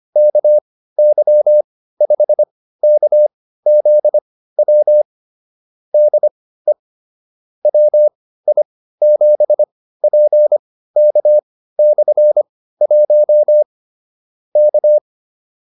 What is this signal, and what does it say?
Morse code, heard as a keyed tone, about 25 wpm, spaced as for 15 wpm, 600 Hz. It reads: KY5KZW DE WI7PK/1 K